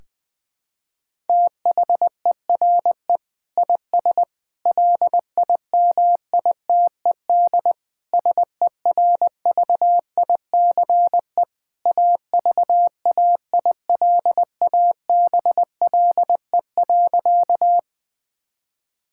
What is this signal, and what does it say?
Morse code, heard as a keyed tone, about 21 wpm, 700 Hz. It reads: THERE IS LIMITED SERVICE AVAILABLE.
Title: Sentences from Top 1000 Words - Review Part 6 - 20wpm